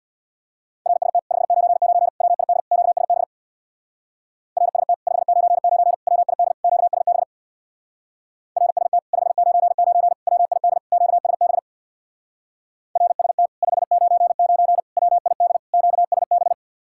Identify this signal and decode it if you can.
Morse code, heard as a keyed tone, about 66 wpm, 700 Hz. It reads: RST 599 WID QSB RST 599 WID QSB RST 599 WID QSB RST 599 WID QSB